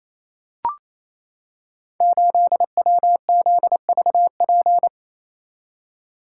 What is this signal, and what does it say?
Morse code, heard as a keyed tone, about 27 wpm, 700 Hz. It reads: 8WZVP
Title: Code Groups - 5 Letter:Numbers - 28wpm